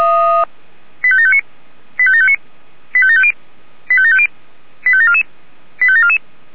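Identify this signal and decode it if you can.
Five-tone selective call (ZVEI), heard as quick runs of five tones, tones of about 70 ms. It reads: 85759 85759 85750 85750 85744 85744
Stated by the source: ZVEI five-tone selective call received on SDR.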